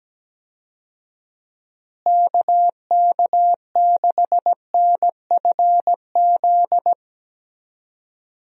Morse code, heard as a keyed tone, about 17 wpm, 700 Hz. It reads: KK6NFZ
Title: USA Call Signs - 6 Characters - Repeated 2x - 17wpm